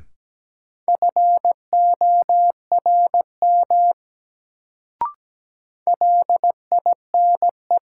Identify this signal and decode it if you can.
Morse code, heard as a keyed tone, about 17 wpm, 700 Hz. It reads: FORM LINE
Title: Sentences from Top 1000 Words - Review Part 1 - New Word Drill - 17wpm